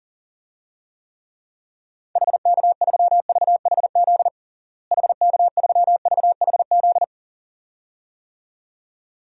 Morse code, heard as a keyed tone, about 37 wpm, 700 Hz. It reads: HK3VHZ HK3VHZ
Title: DXCC-100 Call Signs - 6 Characters - Repeated 2x - 40wpm